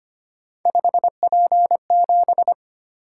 Morse code, heard as a keyed tone, about 25 wpm, 700 Hz. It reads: 5P7
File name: Three Letter-Number Combinations - 25wpm